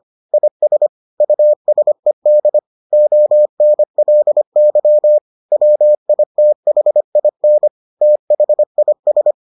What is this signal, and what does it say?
Morse code, heard as a keyed tone, about 25 wpm, 600 Hz. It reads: IS USED ONLY WITHIN THIS